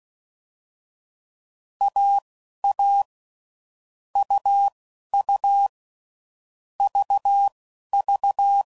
Morse code, heard as a keyed tone, about 16 wpm, spaced as for 11 wpm, 800 Hz. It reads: AA UU VV
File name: Combinations / Difficult Combos / AUV AUV4